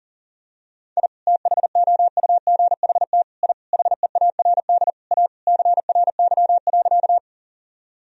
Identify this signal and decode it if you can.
Morse code, heard as a keyed tone, about 40 wpm, 700 Hz. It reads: I THOUGHT I HEARD A CRY.
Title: Sentences from Top 1000 Words - Review Part 2 - 40wpm